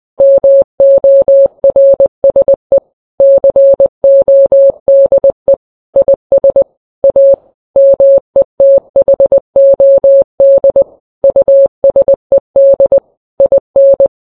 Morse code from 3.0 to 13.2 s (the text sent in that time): CODE IS A METHOD USED